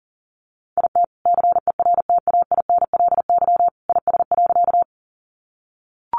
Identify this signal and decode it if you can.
Morse code, heard as a keyed tone, about 40 wpm, 700 Hz. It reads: IT CERTAINLY IS.
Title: Sentences from Top 900 Words 40wpm